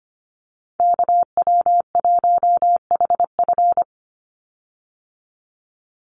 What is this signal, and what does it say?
Morse code, heard as a keyed tone, about 25 wpm, 700 Hz. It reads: KW1HF
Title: USA Call Signs 25wpm